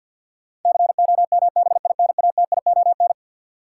Morse code, heard as a keyed tone, about 53 wpm, 700 Hz. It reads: COMBINATION
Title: Sentences from Top 1300 Words - New Word Drill Alt - 50wpm